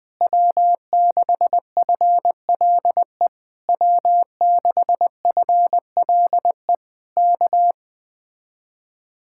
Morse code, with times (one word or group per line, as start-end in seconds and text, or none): 0.2-3.3 W6FLE
3.7-6.7 W6FLE
7.2-7.7 K